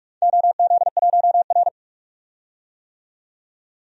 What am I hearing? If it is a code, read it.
Morse code, heard as a keyed tone, about 46 wpm, 700 Hz. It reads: OZ1R
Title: DXCC-100 Call Signs - 4 Characters - 45wpm